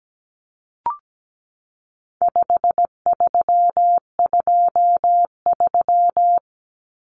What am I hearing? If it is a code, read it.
Morse code, heard as a keyed tone, about 17 wpm, 700 Hz. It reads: 5323
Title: Numbers - 4-Digit 17wpm